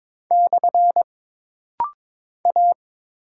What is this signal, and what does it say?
Morse code, heard as a keyed tone, about 22 wpm, 700 Hz. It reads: / A
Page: This rapid-fire timing is geared to push your ICR (Instant Character Recognition) skills.